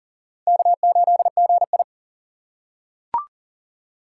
Morse code, heard as a keyed tone, about 40 wpm, 700 Hz. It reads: K8GI
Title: USA Call Signs - 4 Characters - 2x Character Spacing - 40wpm